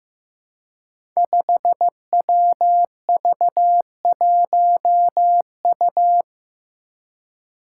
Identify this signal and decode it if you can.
Morse code, heard as a keyed tone, about 15 wpm, 700 Hz. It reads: 5WV1U
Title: Five Challenging Letter-Number Combinations 15wpm